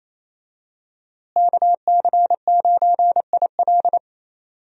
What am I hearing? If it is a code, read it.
Morse code, heard as a keyed tone, about 28 wpm, 700 Hz. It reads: KC9IL
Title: USA Call Signs - 5 Characters - Repeated 2x - 28wpm